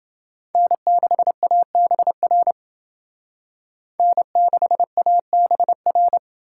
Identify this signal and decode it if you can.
Morse code, heard as a keyed tone, about 29 wpm, 700 Hz. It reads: N6ABR N6ABR